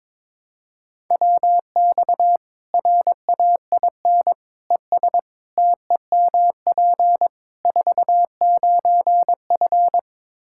Morse code, heard as a keyed tone, about 22 wpm, 700 Hz. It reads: WX RAIN ES TEMP 49F
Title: Practice QSO - Exchange 2 - Info - 22wpm